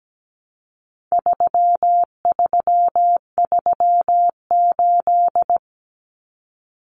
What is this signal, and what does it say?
Morse code, heard as a keyed tone, about 17 wpm, 700 Hz. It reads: 3338
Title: Common Cut Numbers - 4-Digit 17wpm